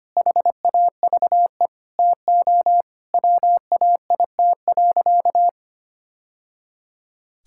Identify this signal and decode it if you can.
Morse code, heard as a keyed tone, about 25 wpm, 700 Hz. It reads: HAVE TO WAIT.
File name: practice / 25wpm / Sentences with 3-Word Phrases - 25wpm